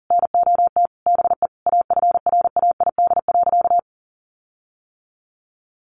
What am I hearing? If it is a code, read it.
Morse code, heard as a keyed tone, about 40 wpm, 700 Hz. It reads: NOT BE AFRAID.
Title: Sentences from Top 1500 Words 40wpm